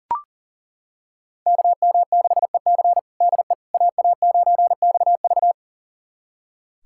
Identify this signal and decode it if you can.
Morse code, heard as a keyed tone, about 40 wpm, 700 Hz. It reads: KM6EC DE AA9XV
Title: USA Callsign DE Callsign - 5 Characters - 40wpm